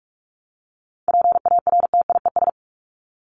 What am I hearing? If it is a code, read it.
Morse code, heard as a keyed tone, about 44 wpm, 700 Hz. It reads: PARTIES
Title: ES:S Suffix - Top 100 Words with Base Words 45wpm